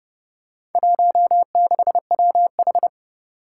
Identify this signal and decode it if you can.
Morse code, heard as a keyed tone, about 28 wpm, 700 Hz. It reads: 16WH